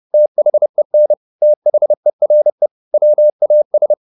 Morse code, heard as a keyed tone, about 30 wpm, 600 Hz. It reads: THEN THERE WAS
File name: the old man and the sea / the old man and the sea 1 30wpm